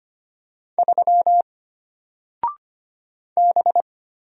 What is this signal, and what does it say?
Morse code, heard as a keyed tone, about 25 wpm, 700 Hz. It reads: 3 B
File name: Single-Letter-Number-25wpm